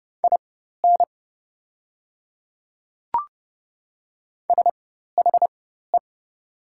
Morse code, heard as a keyed tone, about 30 wpm, spaced as for 14 wpm, 700 Hz. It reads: IN SHE
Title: 4x Character Spacing - Top 100 Words - 30wpm